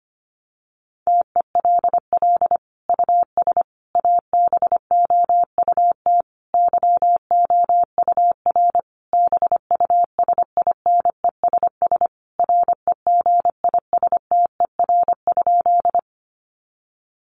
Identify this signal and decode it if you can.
Morse code, heard as a keyed tone, about 25 wpm, 700 Hz. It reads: TELL US ABOUT YOUR BUSINESS REGISTER?
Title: Sentences from Top 1200 Words 25wpm